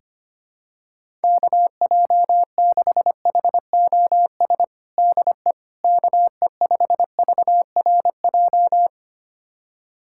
Morse code, heard as a keyed tone, about 25 wpm, 700 Hz. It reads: KJ6HOS DE KE5VRJ
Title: USA Callsign DE Callsign - 6 Characters - 25wpm